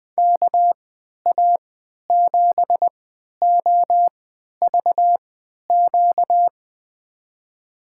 Morse code, heard as a keyed tone, about 20 wpm, spaced as for 11 wpm, 700 Hz. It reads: KA7OVQ
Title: USA Call Signs - 6 Characters - 3x Character Spacing - 20wpm